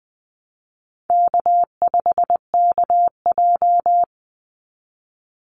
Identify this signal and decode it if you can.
Morse code, heard as a keyed tone, about 20 wpm, 700 Hz. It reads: K5KJ
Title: Easy Call Signs Repeated 2x - 20wpm